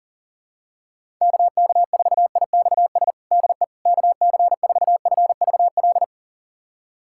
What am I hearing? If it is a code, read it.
Morse code, heard as a keyed tone, about 40 wpm, 700 Hz. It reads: KK4IXS DE KC4FVL
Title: USA Callsign DE Callsign - 6 Characters - 40wpm